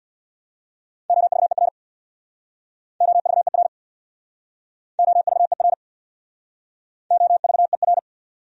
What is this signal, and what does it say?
Morse code, heard as a keyed tone, about 62 wpm, 700 Hz. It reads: OVER OVER OVER OVER
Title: Speed-Racing Top 100 Words 50wpm